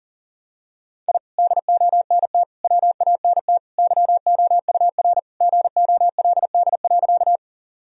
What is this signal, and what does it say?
Morse code, heard as a keyed tone, about 40 wpm, 700 Hz. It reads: I DONT WANT YOUR GOLD.
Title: Sentences from Top 1000 Words - Review Part 3 - 40wpm